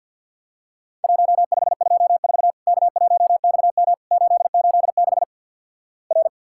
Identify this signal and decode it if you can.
Morse code, heard as a keyed tone, about 50 wpm, 700 Hz. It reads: 1524 K1XM 886 R
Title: Worked All Europe DX Contest - QTC Exchange - 50wpm